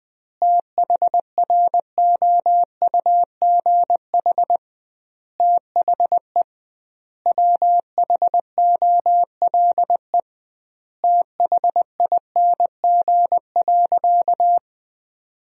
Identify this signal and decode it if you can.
Morse code, heard as a keyed tone, about 20 wpm, 700 Hz. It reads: THROUGH THE WHOLE THING.
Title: 2x Word-Spacing - Sentences from Top 400 Words - 20wpm